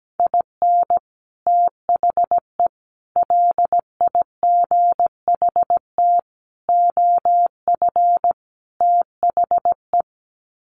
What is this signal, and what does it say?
Morse code, heard as a keyed tone, about 17 wpm, 700 Hz. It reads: IN THE LIGHT OF THE
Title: Sets of 5 Words Encore2 17wpm